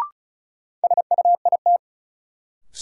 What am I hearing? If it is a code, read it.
Morse code, heard as a keyed tone, about 36 wpm, 700 Hz. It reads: SUIT
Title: Sentences from Top 1600 Words - New Word Drill - 35wpm